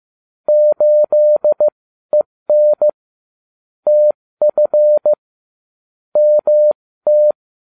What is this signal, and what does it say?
Morse code, heard as a keyed tone, about 15 wpm, spaced as for 13 wpm, 600 Hz. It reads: 8EN TF MT